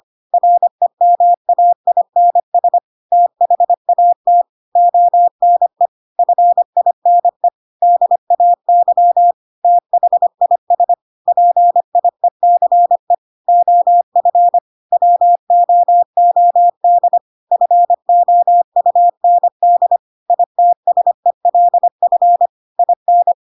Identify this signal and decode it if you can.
Morse code, heard as a keyed tone, about 25 wpm, 700 Hz. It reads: REMAINS THAT ONE FINE DAY THIS PIECE OF WOOD FOUND ITSELF IN
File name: Pinocchio / Pinocchio ch1 25 25 ik1hge